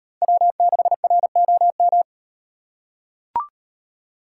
Morse code, heard as a keyed tone, about 38 wpm, 700 Hz. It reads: W6ROM